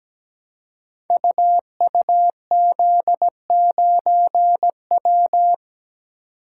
Morse code, heard as a keyed tone, about 17 wpm, 700 Hz. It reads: UUZ9W
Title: Code Groups - 5 Letter:Numbers - 17wpm